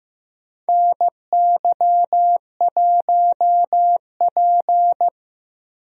Morse code, heard as a keyed tone, about 15 wpm, 700 Hz. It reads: NY1P
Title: USA Call Signs - 4 Characters - Repeated 2x - 15wpm